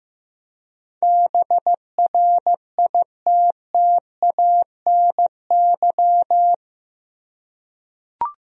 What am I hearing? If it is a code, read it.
Morse code, heard as a keyed tone, about 15 wpm, 700 Hz. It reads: BRITTANY
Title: USA Common Names - 15wpm